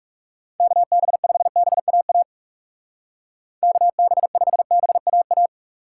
Morse code, heard as a keyed tone, about 37 wpm, 700 Hz. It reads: KB5BAA KB5BAA
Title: Speed Racing - USA 2x3 Call Signs 30wpm